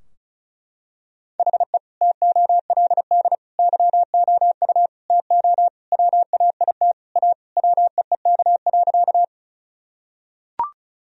Morse code, heard as a keyed tone, about 34 wpm, 700 Hz. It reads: HE TOLD YOU TO WAIT A WEEK.
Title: Sentences from Top 1000 Words - Review Part 2 - 35wpm